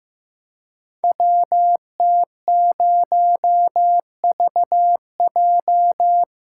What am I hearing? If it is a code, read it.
Morse code, heard as a keyed tone, about 15 wpm, 700 Hz. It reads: WT0VJ
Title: Code Groups - 5 Letter:Numbers - 15wpm